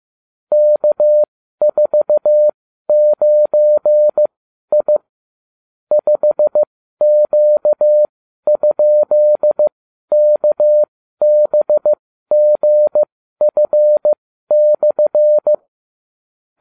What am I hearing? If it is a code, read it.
Morse code, heard as a keyed tone, about 15 wpm, spaced as for 12 wpm, 600 Hz. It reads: K49I 5Q?KBGF/